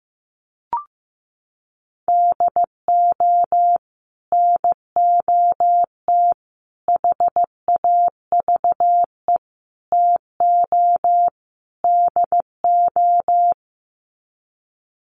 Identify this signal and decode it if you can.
Morse code, heard as a keyed tone, about 15 wpm, 700 Hz. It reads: DO NOT HAVE TO DO
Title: Sets of 5 Words 15wpm